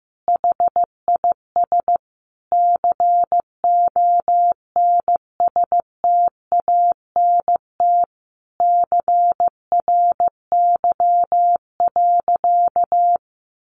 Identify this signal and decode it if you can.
Morse code, heard as a keyed tone, about 15 wpm, 700 Hz. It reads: HIS CONSTANT CRY.
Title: Sentences from Top 1500 Words 15wpm